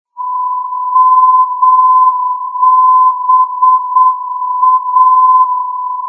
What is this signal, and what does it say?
Morse code, heard as a keyed tone, about 7 wpm, 1020 Hz. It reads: MBA
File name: europe / d-6094 - ndb mansbach mba-289 (2004) (rainer brannolte, d) xc